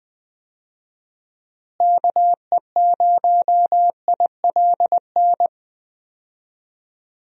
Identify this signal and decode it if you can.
Morse code, heard as a keyed tone, about 20 wpm, 700 Hz. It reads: KE0ILN
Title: USA Call Signs - 6 Characters - Repeated 2x - 20wpm